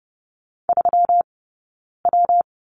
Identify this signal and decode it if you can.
Morse code, heard as a keyed tone, about 30 wpm, 700 Hz. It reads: 3 W
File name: Lesson 20d - Number 3 - Two Characters 30wpm